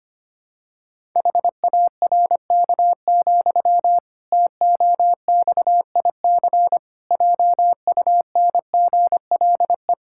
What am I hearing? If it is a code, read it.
Morse code, heard as a keyed tone, about 25 wpm, 700 Hz. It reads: HARK, TOXIC JUNGLE
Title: Pangram Challenge v2 - 25wpm